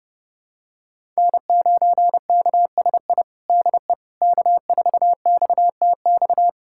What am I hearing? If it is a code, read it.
Morse code, heard as a keyed tone, about 30 wpm, 700 Hz. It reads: N9KSI DE K4XTX
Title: USA Callsign DE Callsign - 5 Characters - 30wpm